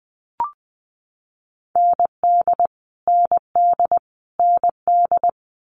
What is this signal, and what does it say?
Morse code, heard as a keyed tone, about 20 wpm, 700 Hz. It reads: ND ND ND
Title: Multi-letter Phonemes Repeated 3x 20wpm